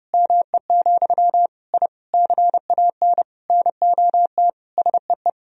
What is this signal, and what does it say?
Morse code, heard as a keyed tone, about 30 wpm, 700 Hz. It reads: ME, I CAN NOT SEE
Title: Sentences from Top 1000 Words - Review Part 3 - 30wpm